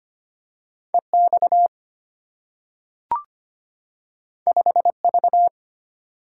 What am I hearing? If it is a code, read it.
Morse code, heard as a keyed tone, about 25 wpm, 700 Hz. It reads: EX 5V
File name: Two Letter-Number Combinations - 25wpm